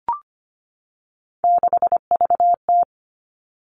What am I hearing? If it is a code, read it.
Morse code, heard as a keyed tone, about 25 wpm, 700 Hz. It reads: 6VT
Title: Three Challenging Letter-Number Combinations 25wpm